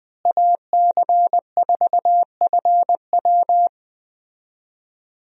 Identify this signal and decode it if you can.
Morse code, heard as a keyed tone, about 21 wpm, 700 Hz. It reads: AC4FW
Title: USA Call Signs - 5 Characters - Repeated 3x - 20wpm